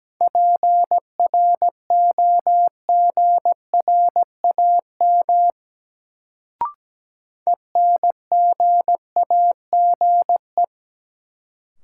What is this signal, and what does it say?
Morse code, heard as a keyed tone, about 17 wpm, 700 Hz. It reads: PROGRAM ENGAGE
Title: Sentences from Top 1500 Words - New Word Drill Alt - 17wpm